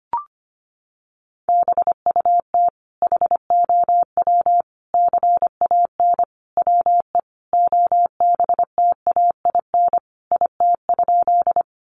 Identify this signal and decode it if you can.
Morse code, heard as a keyed tone, about 25 wpm, 700 Hz. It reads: BUT HOW CAN WE OBTAIN IT?
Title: Sentences from Top 1300 Words 25wpm